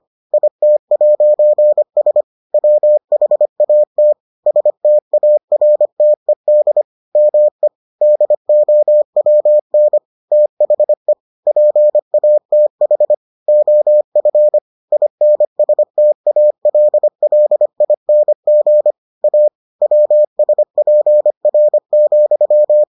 Morse code, heard as a keyed tone, about 25 wpm, 600 Hz. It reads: IT'S WHAT STARTED ME DOWN THE PATH OF INSTALLING A WSPR,